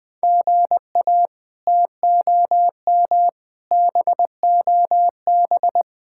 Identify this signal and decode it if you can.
Morse code, heard as a keyed tone, about 20 wpm, 700 Hz. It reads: GA TOM BOB
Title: K1USN SST QSO Practice - 20wpm